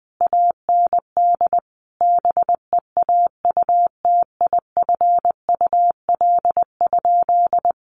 Morse code, heard as a keyed tone, about 20 wpm, 700 Hz. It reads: AND BEAUTIFUL?
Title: Sentences from Top 1200 Words 20wpm